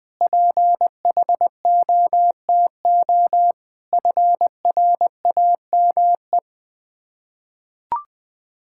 Morse code, heard as a keyed tone, about 20 wpm, 700 Hz. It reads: PHOTO FRAME
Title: Difficult Words with Hints - 20wpm